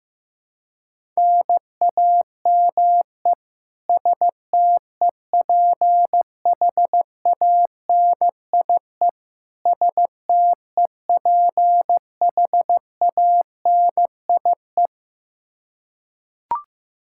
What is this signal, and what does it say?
Morse code, heard as a keyed tone, about 15 wpm, 700 Hz. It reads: NAME STEPHANIE STEPHANIE
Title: Practice QSO - Exchange 1 - Beginning - 15wpm